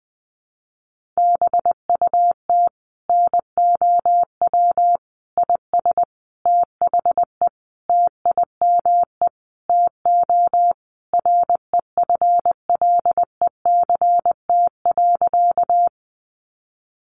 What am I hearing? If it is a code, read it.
Morse code, heard as a keyed tone, about 20 wpm, 700 Hz. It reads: BUT NOW IS THE TIME TO REFLECT.
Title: Sentences from Top 1500 Words 20wpm